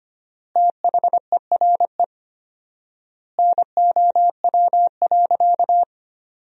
Morse code, heard as a keyed tone, about 25 wpm, 700 Hz. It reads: THERE NOW.
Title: 4x Word-Spacing - Sentences from Top 200 Words - 25wpm